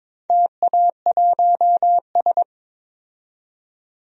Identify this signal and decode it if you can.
Morse code, heard as a keyed tone, about 22 wpm, 700 Hz. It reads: TA1S
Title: DXCC-100 Call Signs - 4 Characters - 22wpm